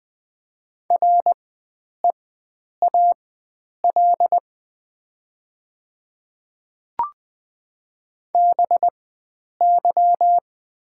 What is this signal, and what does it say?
Morse code, heard as a keyed tone, about 20 wpm, spaced as for 9 wpm, 700 Hz. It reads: REAL BY
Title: 4x Character Spacing - Top 300 Words - 20wpm